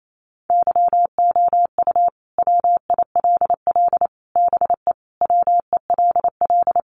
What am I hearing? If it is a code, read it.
Morse code, heard as a keyed tone, about 28 wpm, 700 Hz. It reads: YOU WILL BE WELL
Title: Sentences from Top 1000 Words 28wpm